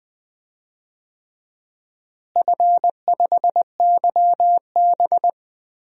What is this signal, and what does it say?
Morse code, heard as a keyed tone, about 20 wpm, 700 Hz. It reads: F5YB
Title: DXCC-100 Call Signs - 4 Characters - Repeated 3x - 20wpm